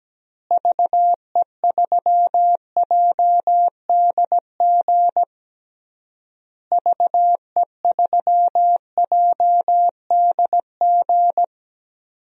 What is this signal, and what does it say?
Morse code, heard as a keyed tone, about 17 wpm, 700 Hz. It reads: VE3JDG VE3JDG
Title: DXCC-100 Call Signs - 6 Characters - Repeated 3x - 17wpm